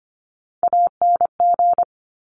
Morse code, heard as a keyed tone, about 25 wpm, 700 Hz. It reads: ANG
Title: Top Three Letter Combinations in Words 25wpm